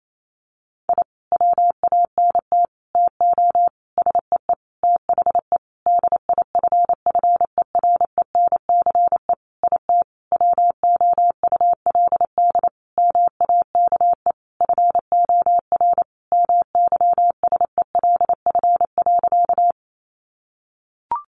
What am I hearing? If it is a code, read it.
Morse code, heard as a keyed tone, about 28 wpm, 700 Hz. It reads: I WANT TO SEE THE DIFFERENCE IT WOULD MAKE FOR MYSELF.
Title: Sentences from Top 1300 Words 28wpm